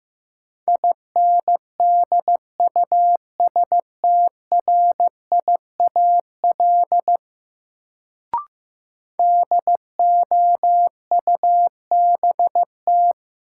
Morse code, heard as a keyed tone, about 15 wpm, 700 Hz. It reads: INDUSTRIAL DOUBT
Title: Sets of 5 Words - New Word Drill - 15wpm